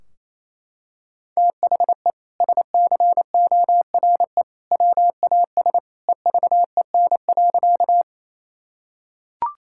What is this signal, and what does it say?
Morse code, heard as a keyed tone, about 28 wpm, 700 Hz. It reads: THE SCORE WAS EVEN.
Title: Sentences from Top 1600 Words 28wpm